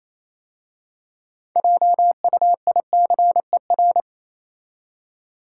Morse code, heard as a keyed tone, about 28 wpm, 700 Hz. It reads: JUICER